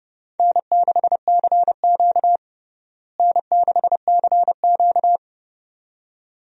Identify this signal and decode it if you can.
Morse code, heard as a keyed tone, about 32 wpm, 700 Hz. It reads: N6CQ N6CQ
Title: USA Call Signs - 4 Characters - Repeated 2x - 30wpm